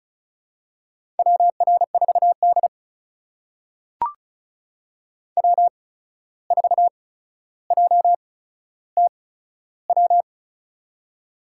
Morse code, heard as a keyed tone, about 36 wpm, spaced as for 14 wpm, 700 Hz. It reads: WR4D W4JTW
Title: USA Call Signs - 8x Character Spacing - 35wpm